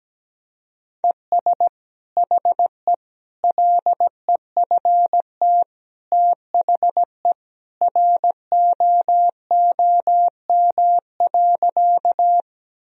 Morse code, heard as a keyed tone, about 17 wpm, 700 Hz. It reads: ES HE LEFT THE ROOM.
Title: Sentences from Top 300 Words and Abbreviations 17wpm